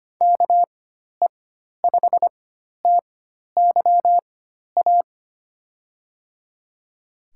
Morse code, heard as a keyed tone, about 25 wpm, spaced as for 12 wpm, 700 Hz. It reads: KE5TYA
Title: USA Call Signs - 6 Characters - 4x Character Spacing - 25wpm